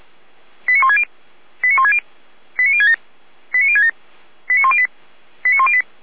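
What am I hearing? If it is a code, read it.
Five-tone selective call (ZVEI), heard as quick runs of five tones, tones of about 80 ms. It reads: 89160 89160 89067 89067 89108 89108
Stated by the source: ZVEI five-tone selective call received by SDR.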